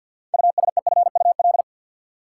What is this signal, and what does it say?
Morse code, heard as a keyed tone, about 47 wpm, 700 Hz. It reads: USEFUL
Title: Sentences from Top 700 Words - New Word Drill - 50wpm